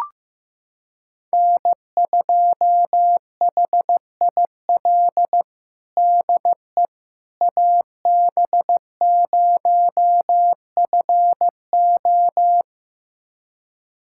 Morse code, heard as a keyed tone, about 15 wpm, 700 Hz. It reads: N2HIL DE AB0FO
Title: USA Callsign DE Callsign - 5 Characters - 15wpm